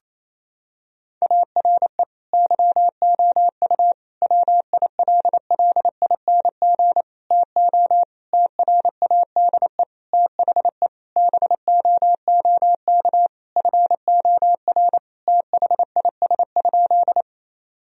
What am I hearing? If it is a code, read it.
Morse code, heard as a keyed tone, about 27 wpm, 700 Hz. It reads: ARE YOU WILLING TO TRADE THE BOOK FOR THIS?
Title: Sentences from Top 1000 Words - Review Part 4 - 28wpm